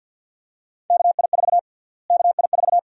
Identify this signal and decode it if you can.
Morse code, heard as a keyed tone, about 46 wpm, 700 Hz. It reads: KI4 KI4
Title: USA Call Sign Prefixes - Repeated 2x - 50wpm